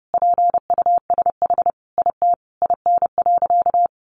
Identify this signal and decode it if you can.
Morse code, heard as a keyed tone, about 30 wpm, 700 Hz. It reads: PUSH IT IN.
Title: Sentences from Top 1300 Words 30wpm